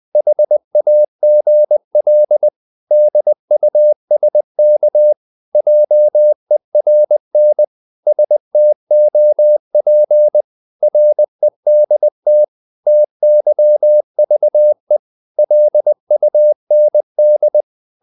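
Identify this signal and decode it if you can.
Morse code, heard as a keyed tone, about 20 wpm, 600 Hz. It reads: HAGL DUSK JERN STOP REDT TYVE LUND
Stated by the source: Morse code practice of short Danish words at 20 wpm, from CW med Gnister.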